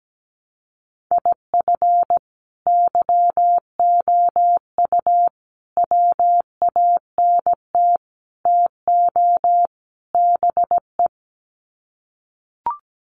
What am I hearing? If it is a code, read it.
Morse code, heard as a keyed tone, about 17 wpm, 700 Hz. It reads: IF YOU WANT TO BE
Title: Sets of 5 Words Encore 17wpm